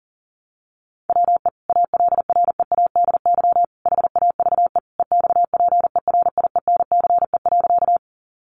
Morse code, heard as a keyed tone, about 40 wpm, 700 Hz. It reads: WE ALREADY HAVE EXPERIENCE.
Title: Sentences from Top 700 Words 40wpm